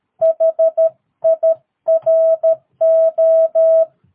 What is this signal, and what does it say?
Morse code, heard as a keyed tone, about 13 wpm, 650 Hz. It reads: HIRO
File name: jh8cbh / 2022cw